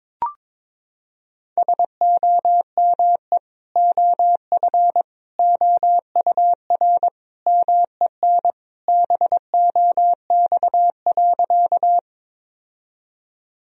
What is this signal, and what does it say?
Morse code, heard as a keyed tone, about 21 wpm, 700 Hz. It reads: SOME OF OUR MEN BOX.
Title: Sentences from Top 1000 Words - Review Part 2 - 22wpm